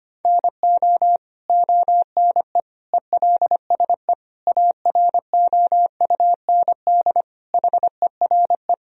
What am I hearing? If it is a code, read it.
Morse code, heard as a keyed tone, about 25 wpm, 700 Hz. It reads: NO ONE ELSE AROUND HERE
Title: Sentences from Top 1000 Words - Review Part 3 - 25wpm